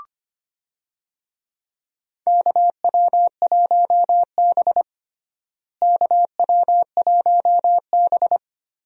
Morse code, heard as a keyed tone, about 25 wpm, 700 Hz. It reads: KW1B KW1B
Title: USA Call Signs - 4 Characters - Repeated 3x - 25wpm